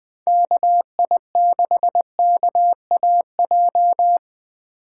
Morse code, heard as a keyed tone, about 20 wpm, 700 Hz. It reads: KI6KAJ
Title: USA Call Signs - 3x Character Spacing - 20wpm